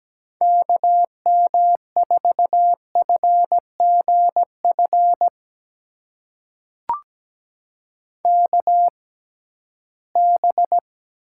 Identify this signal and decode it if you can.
Morse code, heard as a keyed tone, about 17 wpm, spaced as for 13 wpm, 700 Hz. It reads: KM4FGF KB
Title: USA Call Signs - 6 Characters - 6x Character Spacing - 17wpm